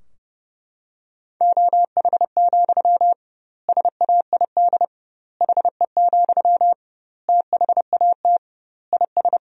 Morse code, heard as a keyed tone, about 32 wpm, 700 Hz. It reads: OH, SAID HE, THAT IS
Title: 2x Word-Spacing - Sentences from Top 500 Words - 30wpm